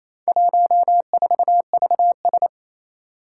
Morse code, heard as a keyed tone, about 27 wpm, 700 Hz. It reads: 14VS
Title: Four Challenging Letter-Number Combinations 28wpm